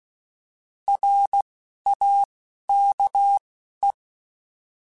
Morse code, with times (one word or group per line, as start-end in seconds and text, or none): 0.9-1.4 R
1.9-2.2 A
2.7-3.4 K
3.8-3.9 E